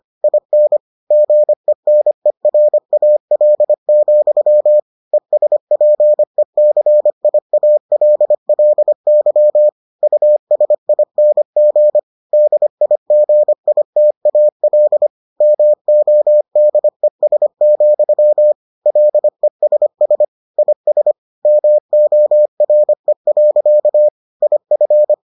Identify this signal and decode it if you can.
Morse code, heard as a keyed tone, about 25 wpm, 600 Hz. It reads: IN GENERAL, ESPECIALLY USING DIGITAL MODES, LESS IS MORE. IF